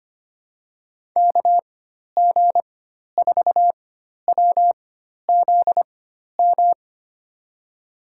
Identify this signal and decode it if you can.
Morse code, heard as a keyed tone, about 25 wpm, spaced as for 12 wpm, 700 Hz. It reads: KG4WZM
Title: USA Call Signs - 6 Characters - 4x Character Spacing - 25wpm